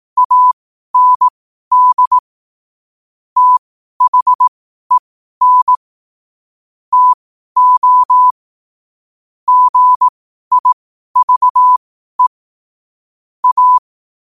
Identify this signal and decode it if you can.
Morse code, heard as a keyed tone, about 18 wpm, spaced as for 12 wpm, 1000 Hz. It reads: AND THEN TO GIVE A
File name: qotdfiles / 12wpm